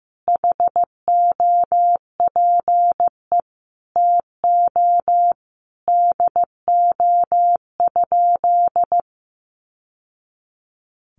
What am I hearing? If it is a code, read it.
Morse code, heard as a keyed tone, about 15 wpm, 700 Hz. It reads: HOPE TO DO?
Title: Sentences from Top 1000 Words 15wpm